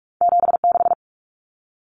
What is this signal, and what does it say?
Morse code, heard as a keyed tone, about 46 wpm, 700 Hz. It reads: 76
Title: Numbers - 2-Digit 45wpm